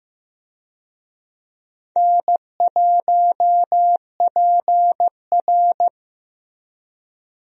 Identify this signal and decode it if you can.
Morse code, heard as a keyed tone, about 15 wpm, 700 Hz. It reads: N1PR
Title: USA Call Signs - 4 Characters - Repeated 2x - 15wpm